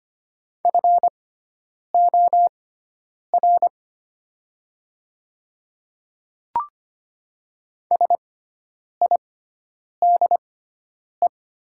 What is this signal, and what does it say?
Morse code, heard as a keyed tone, about 25 wpm, spaced as for 9 wpm, 700 Hz. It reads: FOR SIDE
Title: 6x Character Spacing - Top 200 Words - 25wpm